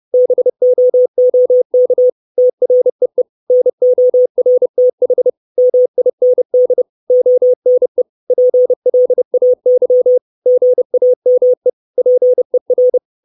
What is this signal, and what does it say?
Morse code, heard as a keyed tone, about 30 wpm, 500 Hz. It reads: BOOK TREE NORTH MIND ONE PLAY GAME PER
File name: 632 Words 30WPM